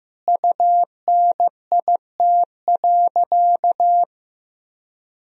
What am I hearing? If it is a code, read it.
Morse code, heard as a keyed tone, about 16 wpm, 700 Hz. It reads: UNIT.
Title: Sentences from Top 1000 Words - Review Part 2 - 15wpm